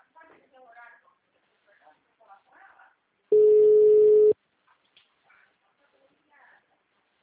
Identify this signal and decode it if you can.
Ringback tone (Europe, 425 Hz), with speech.
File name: out- -2040- - - .402